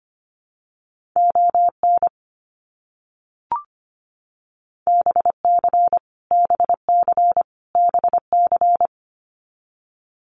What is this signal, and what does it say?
Morse code, heard as a keyed tone, about 25 wpm, 700 Hz. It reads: ON BC BC BC
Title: Canadian Provinces and Territories - Repeated 3x - 25wpm